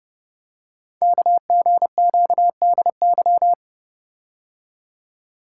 Morse code, heard as a keyed tone, about 30 wpm, 700 Hz. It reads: KGQDY
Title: Code Groups - 5 Letters - 30wpm